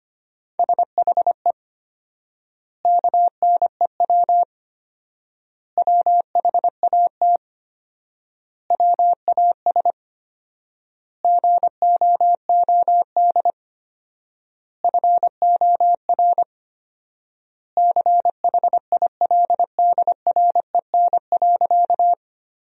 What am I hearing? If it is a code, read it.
Morse code, heard as a keyed tone, about 25 wpm, 700 Hz. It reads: SHE KNEW WHAT WAS GOOD FOR CHILDREN.
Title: 4x Word-Spacing - Sentences from Top 400 Words - 25wpm